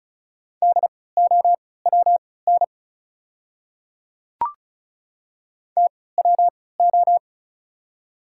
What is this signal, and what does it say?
Morse code, heard as a keyed tone, about 36 wpm, spaced as for 20 wpm, 690 Hz. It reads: DOWN TWO
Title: 3x Character Spacing - Top 100 Words - 35wpm